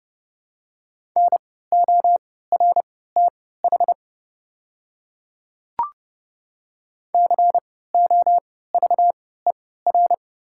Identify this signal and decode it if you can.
Morse code, heard as a keyed tone, about 32 wpm, spaced as for 17 wpm, 700 Hz. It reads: NORTH COVER
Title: 3x Character Spacing - Top 500 Words - 30wpm